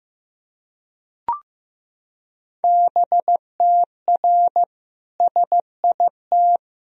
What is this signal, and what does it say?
Morse code, heard as a keyed tone, about 15 wpm, 700 Hz. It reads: BTR SIT
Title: Sentences from Top 500 Words and Abbreviations 15wpm